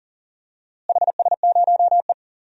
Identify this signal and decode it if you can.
Morse code, heard as a keyed tone, about 40 wpm, 700 Hz. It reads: HS0E
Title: DXCC-100 Call Signs - 4 Characters - Repeated 3x - 40wpm